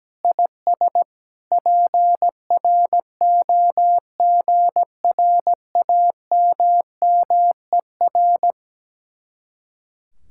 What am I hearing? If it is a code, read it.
Morse code, heard as a keyed tone, about 17 wpm, 700 Hz. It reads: IS PROGRAMMER
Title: Speed-Racing Common QSO Phrases - 17wpm